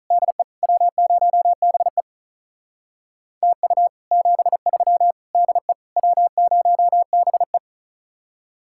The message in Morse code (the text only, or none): DE W0BE TU 73 DE W0BE